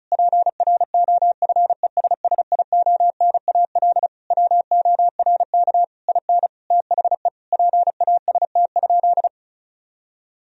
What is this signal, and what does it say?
Morse code, heard as a keyed tone, about 36 wpm, 700 Hz. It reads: PROFESSIONAL WORK IN THE PAST?
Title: Sentences from Top 1000 Words - Review Part 4 - 35wpm